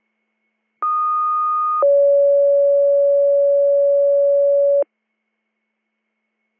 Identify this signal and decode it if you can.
Two-tone page, heard as a one-second tone, then a three-second tone, about 1230 then 560 Hz. It is heard from 0.8 to 4.8 s.